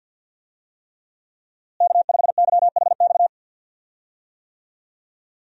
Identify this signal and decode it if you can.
Morse code, heard as a keyed tone, about 47 wpm, 700 Hz. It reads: K5YHX
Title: USA Call Signs - 5 Characters - Repeated 3x - 50wpm